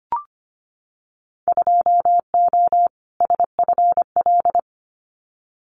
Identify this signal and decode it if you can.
Morse code, heard as a keyed tone, about 25 wpm, 700 Hz. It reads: 2O SFL
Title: Winter Field Day Exchanges 25wpm